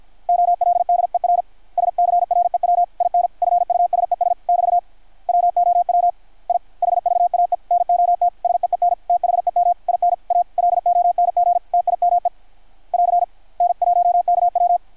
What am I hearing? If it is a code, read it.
Morse code, heard as a keyed tone, about 53 wpm, 660 Hz. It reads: OLDER SCREW IN FUSES <BT> WOW I HVAE NOT SEEN THEM IN A LONG TIME <AR> N1KW
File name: archived / CRSnet